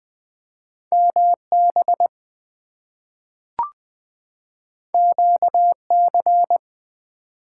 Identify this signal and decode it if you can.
Morse code, heard as a keyed tone, about 20 wpm, 700 Hz. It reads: MB QC
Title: US States and Canadian Provinces 20wpm